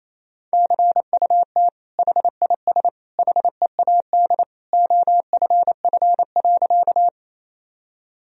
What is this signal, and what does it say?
Morse code, heard as a keyed tone, about 28 wpm, 700 Hz. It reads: CUT HIS HEAD OFF.
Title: Sentences from Top 1000 Words - Review Part 1 - 28wpm